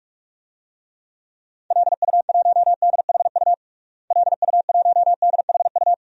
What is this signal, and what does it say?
Morse code, heard as a keyed tone, about 46 wpm, 700 Hz. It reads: LU1DHU LU1DHU
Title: DXCC-100 Call Signs - 6 Characters - Repeated 2x - 45wpm